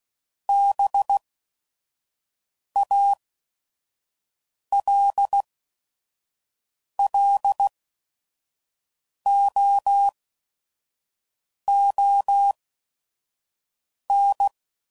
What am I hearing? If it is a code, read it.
Morse code, heard as a keyed tone, about 16 wpm, spaced as for 5 wpm, 800 Hz. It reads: BALLOON